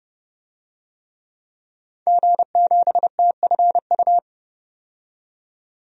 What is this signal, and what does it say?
Morse code, heard as a keyed tone, about 30 wpm, 700 Hz. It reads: G7TFU
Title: DXCC-100 Call Signs - 5 Characters - 30wpm